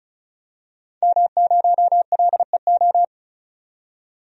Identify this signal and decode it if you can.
Morse code, heard as a keyed tone, about 34 wpm, 700 Hz. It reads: M0LEO
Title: DXCC-100 Call Signs - 5 Characters - 35wpm